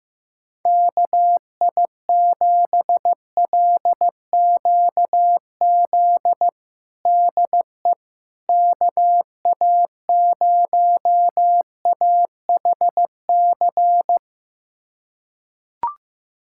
Morse code, heard as a keyed tone, about 16 wpm, 700 Hz. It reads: KI7LQZ DE KA0AHC
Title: USA Callsign DE Callsign 15wpm